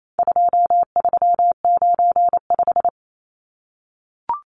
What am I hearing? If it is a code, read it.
Morse code, heard as a keyed tone, about 28 wpm, 700 Hz. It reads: 2395
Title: Numbers - 4-Digit 28wpm